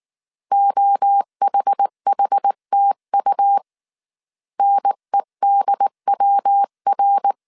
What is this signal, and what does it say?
Morse code, heard as a keyed tone, about 19 wpm, 800 Hz. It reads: OHHTU NEDWR
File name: CW / DET / 1 3 1500C S B 18wpm 1916